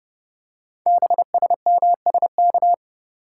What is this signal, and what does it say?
Morse code, heard as a keyed tone, about 30 wpm, 700 Hz. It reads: BSMSK
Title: Code Groups - 5 Letters - 30wpm